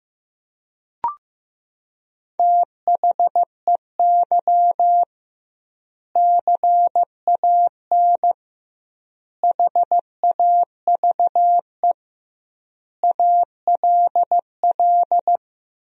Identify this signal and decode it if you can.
Morse code, heard as a keyed tone, about 15 wpm, 700 Hz. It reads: THEY CAN HAVE ALL
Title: 2x Word-Spacing - Sentences from Top 400 Words - 15wpm